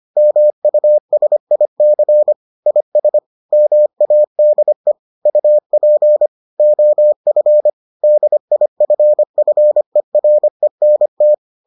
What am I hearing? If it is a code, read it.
Morse code, heard as a keyed tone, about 25 wpm, 600 Hz. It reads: MUSIC IS MADE UP OF DIFFERENT